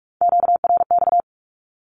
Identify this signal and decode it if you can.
Morse code, heard as a keyed tone, about 46 wpm, 700 Hz. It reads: QRX
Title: Q-Codes with Spoken Meaning - 45wpm